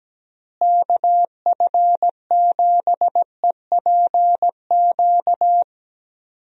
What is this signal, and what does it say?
Morse code, heard as a keyed tone, about 17 wpm, 700 Hz. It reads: KF7EPQ
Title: USA Call Signs - 6 Characters - Repeated 2x - 17wpm